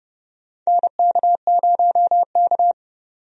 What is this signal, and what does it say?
Morse code, heard as a keyed tone, about 30 wpm, 700 Hz. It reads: NK0K
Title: USA Call Signs - 4 Characters - 30wpm